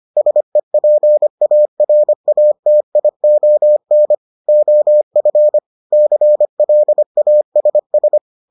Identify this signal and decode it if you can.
Morse code, heard as a keyed tone, about 25 wpm, 600 Hz. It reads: SEPARATION OF CLASS